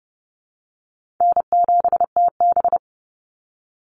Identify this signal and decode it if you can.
Morse code, heard as a keyed tone, about 30 wpm, 700 Hz. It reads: N7TB
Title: Easy Call Signs Repeated 3x - 30wpm